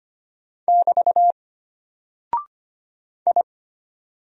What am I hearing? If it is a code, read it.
Morse code, heard as a keyed tone, about 25 wpm, 700 Hz. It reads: <BT> I
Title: Single Character (Letter:Number:Punctuation:Prosign) Alt 25wpm